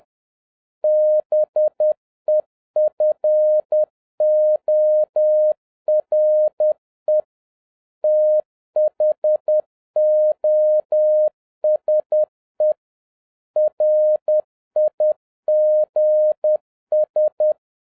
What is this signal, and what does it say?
Morse code, heard as a keyed tone, about 10 wpm, 610 Hz. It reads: BEFORE THOSE RIGS
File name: 2011-07-mp3 / n46 JOHN K3TN 10wpm